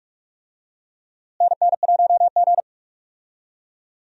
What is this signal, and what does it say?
Morse code, heard as a keyed tone, about 46 wpm, 700 Hz. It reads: NN1G
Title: USA Call Signs - 4 Characters - Repeated 3x - 45wpm